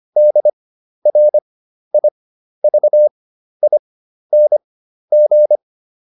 Morse code, heard as a keyed tone, about 25 wpm, spaced as for 12 wpm, 600 Hz. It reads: DRIVING